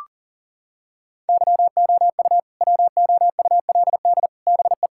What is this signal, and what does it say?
Morse code, heard as a keyed tone, about 40 wpm, 700 Hz. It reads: YOU WOULD BE
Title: Sentences from Top 1000 Words - Review Part 6 - 40wpm